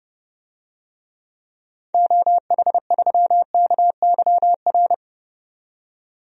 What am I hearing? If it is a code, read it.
Morse code, heard as a keyed tone, about 30 wpm, 700 Hz. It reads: OH3KYR
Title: DXCC-100 Call Signs - 6 Characters - 30wpm